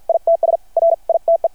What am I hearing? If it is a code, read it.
Morse code, heard as a keyed tone, about 53 wpm, spaced as for 44 wpm, 670 Hz. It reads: ITS RITE
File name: archived / CRSnet